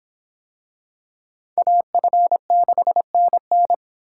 Morse code, heard as a keyed tone, about 24 wpm, 700 Hz. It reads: AF6NN